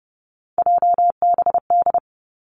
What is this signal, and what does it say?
Morse code, heard as a keyed tone, about 30 wpm, 700 Hz. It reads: JBD
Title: Three Challenging Letter-Number Combinations 30wpm